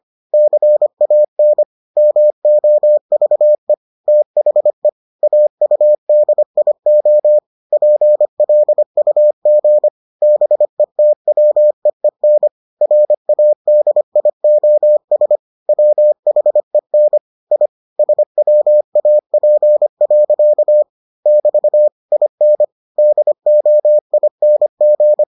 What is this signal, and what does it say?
Morse code, heard as a keyed tone, about 25 wpm, 600 Hz. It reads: CAN MOVE THE AUDIO PLUG BETWEEN RADIOS WHEN I SWAP. <BT> IN DOING